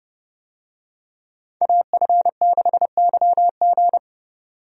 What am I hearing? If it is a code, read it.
Morse code, heard as a keyed tone, about 30 wpm, 700 Hz. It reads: AF6YG
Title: USA Call Signs - 5 Characters - 30wpm